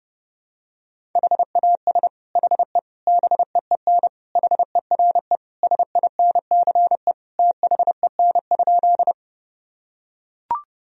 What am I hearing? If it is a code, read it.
Morse code, heard as a keyed tone, about 28 wpm, 700 Hz. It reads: HAS HE BEEN HERE SINCE THEN?
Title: Sentences from Top 1000 Words - Review Part 1 - 30wpm